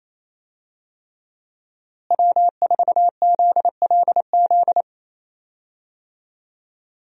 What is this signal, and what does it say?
Morse code, heard as a keyed tone, about 27 wpm, 700 Hz. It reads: W4ZLZ
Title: USA Call Signs - 5 Characters - Repeated 3x - 28wpm